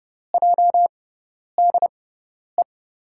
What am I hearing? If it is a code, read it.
Morse code, heard as a keyed tone, about 32 wpm, spaced as for 10 wpm, 700 Hz. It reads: JDE
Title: USA Call Signs - 5 Characters - 6x Character Spacing - 30wpm